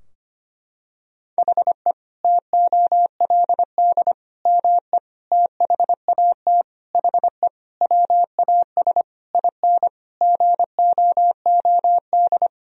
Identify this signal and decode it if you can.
Morse code, heard as a keyed tone, about 25 wpm, 700 Hz. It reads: HE TOLD ME THAT HE WAS IN GOOD
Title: Sentences from Top 1000 Words - Review Part 3 - 25wpm